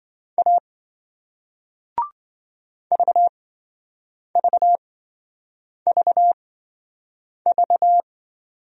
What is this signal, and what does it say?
Morse code, heard as a keyed tone, about 28 wpm, spaced as for 7 wpm, 700 Hz. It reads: A VVVV